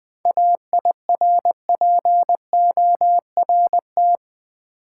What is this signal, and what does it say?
Morse code, heard as a keyed tone, about 20 wpm, 700 Hz. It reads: AIRPORT
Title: Seven Letter Words - 20wpm